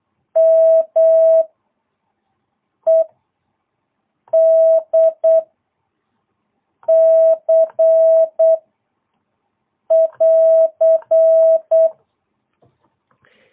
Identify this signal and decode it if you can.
Morse code, heard as a keyed tone, about 8 wpm, 660 Hz. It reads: M E D C <AR>
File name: jh8cbh / 2022cw